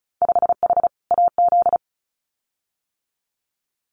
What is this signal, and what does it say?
Morse code, heard as a keyed tone, about 34 wpm, 700 Hz. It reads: 5H AZ
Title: Winter Field Day Exchanges Encore 35wpm